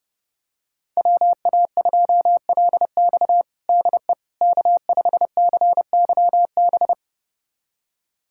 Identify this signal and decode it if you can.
Morse code, heard as a keyed tone, about 30 wpm, 700 Hz. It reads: WA2LX DE K5CYB
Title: USA Callsign DE Callsign - 5 Characters - 30wpm